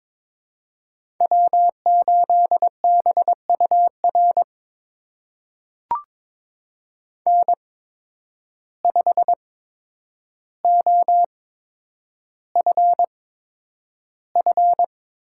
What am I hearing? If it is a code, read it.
Morse code, heard as a keyed tone, about 22 wpm, spaced as for 9 wpm, 700 Hz. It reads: W8BUR N5OFF
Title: USA Call Signs - 5 Characters - 8x Character Spacing - 22wpm